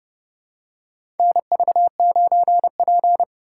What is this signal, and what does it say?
Morse code, heard as a keyed tone, about 30 wpm, 700 Hz. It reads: NV9P
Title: USA Call Signs - 3x Character Spacing - 30wpm